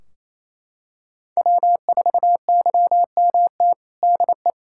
Morse code, heard as a keyed tone, about 27 wpm, 700 Hz. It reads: W4YMT DE
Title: Practice QSO - Exchange 4 - Ending - 28wpm